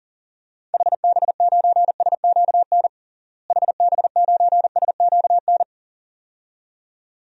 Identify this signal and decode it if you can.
Morse code, heard as a keyed tone, about 40 wpm, 700 Hz. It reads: HB9SQN HB9SQN
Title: DXCC-100 Call Signs - 6 Characters - Repeated 2x - 40wpm